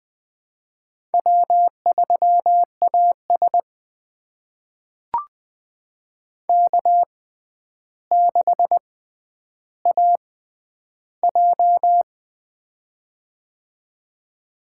Morse code, heard as a keyed tone, about 20 wpm, spaced as for 10 wpm, 700 Hz. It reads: W3AS K6AJ
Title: USA Call Signs - 4 Characters - 6x Character Spacing - 20wpm